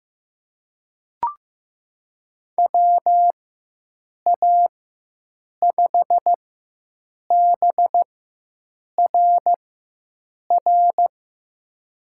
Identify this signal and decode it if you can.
Morse code, heard as a keyed tone, about 15 wpm, spaced as for 7 wpm, 700 Hz. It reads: WA5BRR
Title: USA Call Signs - 6 Characters - 4x Character Spacing - 15wpm